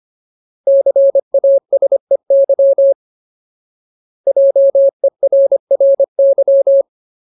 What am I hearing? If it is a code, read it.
Morse code, heard as a keyed tone, about 25 wpm, 550 Hz. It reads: CASEY JERRY
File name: Ultimate Word List / 25WPM Ultimate